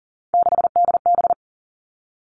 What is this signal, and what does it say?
Morse code, heard as a keyed tone, about 41 wpm, 700 Hz. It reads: 6DB
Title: Three Challenging Letter-Number Combinations 40wpm